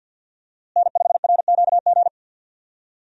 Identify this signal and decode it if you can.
Morse code, heard as a keyed tone, about 47 wpm, 700 Hz. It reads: N5RQG
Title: USA Call Signs - 5 Characters - Repeated 3x - 50wpm